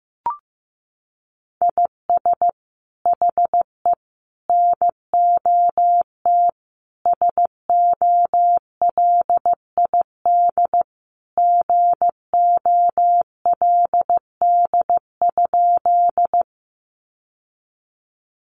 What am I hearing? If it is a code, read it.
Morse code, heard as a keyed tone, about 15 wpm, 700 Hz. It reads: IS HE NOT SOLID GOLD?
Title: Sentences from Top 1500 Words 15wpm